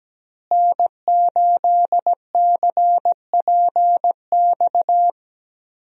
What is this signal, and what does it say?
Morse code, heard as a keyed tone, about 17 wpm, 700 Hz. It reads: N8CPX
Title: USA Call Signs - 5 Characters - Repeated 3x - 17wpm